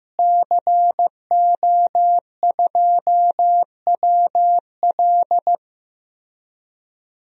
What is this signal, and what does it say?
Morse code, heard as a keyed tone, about 15 wpm, 700 Hz. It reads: CO2WL
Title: DXCC-100 Call Signs - 5 Characters - Repeated 3x - 15wpm